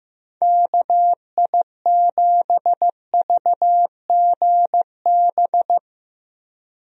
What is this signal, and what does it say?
Morse code, heard as a keyed tone, about 15 wpm, 700 Hz. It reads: KI7VGB
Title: USA Call Signs - 6 Characters - Repeated 2x - 15wpm